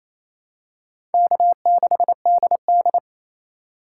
Morse code, heard as a keyed tone, about 28 wpm, 700 Hz. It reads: K6DD
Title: USA Call Signs - 4 Characters - 2x Character Spacing - 28wpm